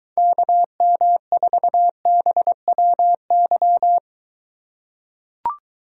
Morse code, heard as a keyed tone, about 23 wpm, 700 Hz. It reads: KM4BWY